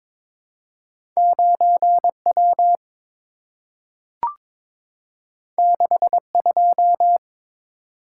Morse code, heard as a keyed tone, about 22 wpm, 700 Hz. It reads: 9W 62
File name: practice / 22wpm / Two Letter-Number Combinations - 22wpm